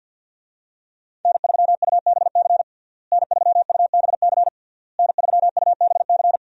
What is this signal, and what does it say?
Morse code, heard as a keyed tone, about 52 wpm, 700 Hz. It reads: N3UBC N3UBC N3UBC
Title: USA Call Signs - 5 Characters - Repeated 3x - 50wpm